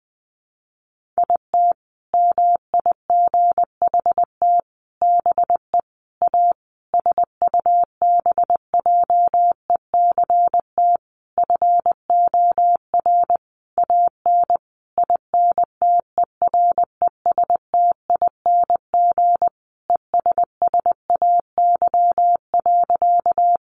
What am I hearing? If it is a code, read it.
Morse code, heard as a keyed tone, about 20 wpm, 700 Hz. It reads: IT MIGHT BE A SUBJECT FOR AN INTERESTING ESSAY.
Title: Sentences from Top 1500 Words 20wpm